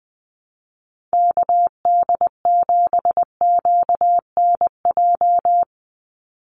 Morse code, heard as a keyed tone, about 20 wpm, 700 Hz. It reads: KD7QNJ
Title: USA Call Signs 20wpm